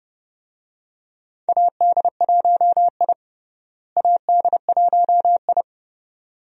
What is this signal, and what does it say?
Morse code, heard as a keyed tone, about 30 wpm, 700 Hz. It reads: AD1I AD1I
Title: USA Call Signs - 4 Characters - Repeated 2x - 30wpm